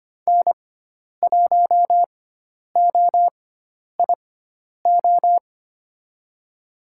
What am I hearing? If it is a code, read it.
Morse code, heard as a keyed tone, about 25 wpm, spaced as for 10 wpm, 700 Hz. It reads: N1OIO
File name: Lesson 16g - Letter c - Call Signs 25wpm